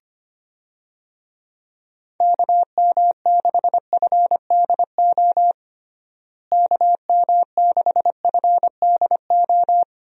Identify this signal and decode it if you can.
Morse code, heard as a keyed tone, about 25 wpm, 700 Hz. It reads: KM6FDO KM6FDO
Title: USA Call Signs - 6 Characters - Repeated 2x - 25wpm